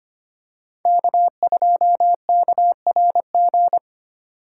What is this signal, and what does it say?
Morse code, heard as a keyed tone, about 25 wpm, 700 Hz. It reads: K2KRG
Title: USA Call Signs - 5 Characters - 4x Character Spacing - 25wpm